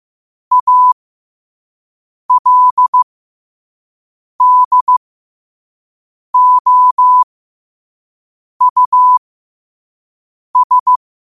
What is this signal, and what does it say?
Morse code, heard as a keyed tone, about 15 wpm, spaced as for 5 wpm, 1000 Hz. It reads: ALDOUS
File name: qotdfiles / 5wpm